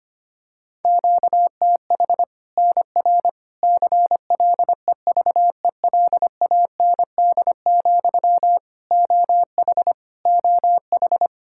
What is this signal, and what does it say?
Morse code, heard as a keyed tone, about 25 wpm, 700 Hz. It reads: QTH NR CLEVELAND, OH OH
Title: Practice QSO - Exchange 1 - Beginning - 25wpm